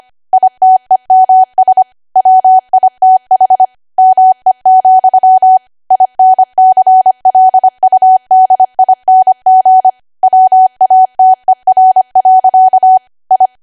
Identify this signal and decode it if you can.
Morse code, heard as a keyed tone, about 25 wpm, 750 Hz. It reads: ITEMS WITH ME, INCLUDING WATER. I
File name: Archive / 25 WPM / 25WPM